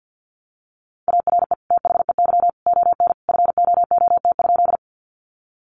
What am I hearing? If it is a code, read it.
Morse code, heard as a keyed tone, about 50 wpm, 700 Hz. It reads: ARE THEY ON FOOT?
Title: Sentences from Top 500 Words 50wpm